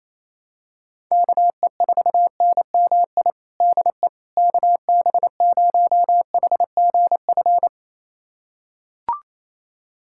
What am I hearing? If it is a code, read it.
Morse code, heard as a keyed tone, about 27 wpm, 700 Hz. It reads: KE4NMI DE KB0HGF
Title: USA Callsign DE Callsign - 6 Characters - 28wpm